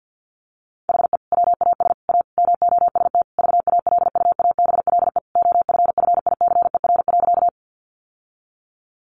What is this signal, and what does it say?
Morse code, heard as a keyed tone, about 50 wpm, 700 Hz. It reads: HE WAS A MOST VALUABLE OFFICER.
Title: Sentences from Top 1200 Words 50wpm